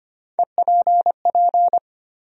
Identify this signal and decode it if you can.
Morse code, heard as a keyed tone, about 25 wpm, 700 Hz. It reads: EPP
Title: USA Call Sign Postfixes - 3 Letters 25wpm